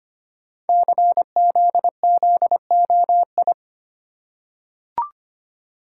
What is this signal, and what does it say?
Morse code, heard as a keyed tone, about 25 wpm, 700 Hz. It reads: CZZOI
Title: Code Groups - 5 Letters - 25wpm